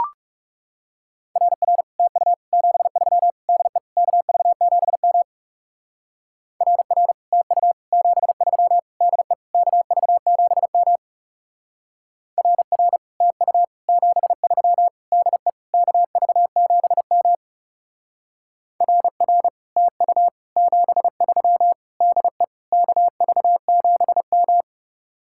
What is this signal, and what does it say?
Morse code, heard as a keyed tone, about 36 wpm, 700 Hz. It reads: RR TU 73 DE KV7M RR TU 73 DE KV7M RR TU 73 DE KV7M RR TU 73 DE KV7M
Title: Speed-Racing Common QSO Phrases - 30wpm